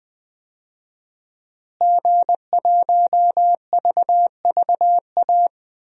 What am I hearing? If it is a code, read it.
Morse code, heard as a keyed tone, about 20 wpm, 700 Hz. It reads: G1VVA
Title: DXCC-100 Call Signs - 5 Characters - Repeated 2x - 20wpm